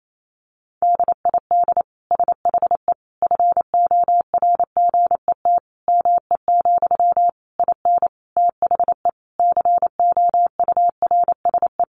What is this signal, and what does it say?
Morse code, heard as a keyed tone, about 28 wpm, 700 Hz. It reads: DID SHE FORGET ME, IN THE COURSE
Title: Sentences from Top 1200 Words 28wpm